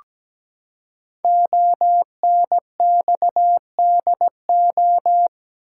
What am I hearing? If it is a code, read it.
Morse code, heard as a keyed tone, about 17 wpm, 700 Hz. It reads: ONXDO
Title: Code Groups - 5 Letters - 17wpm